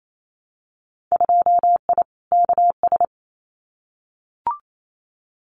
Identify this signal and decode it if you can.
Morse code, heard as a keyed tone, about 28 wpm, 700 Hz. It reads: 2I KS
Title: Winter Field Day Exchanges Encore 28wpm